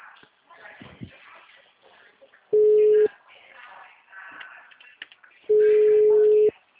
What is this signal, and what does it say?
Ringback tone (Europe, 425 Hz), with speech.